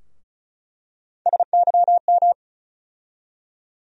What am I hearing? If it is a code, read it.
Morse code, heard as a keyed tone, about 36 wpm, 700 Hz. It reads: SYM
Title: USA Call Sign Postfixes - 3 Letters 35wpm